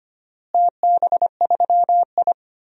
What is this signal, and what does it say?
Morse code, heard as a keyed tone, about 25 wpm, 700 Hz. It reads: TB3I
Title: DXCC-100 Call Signs - 4 Characters - Repeated 3x - 25wpm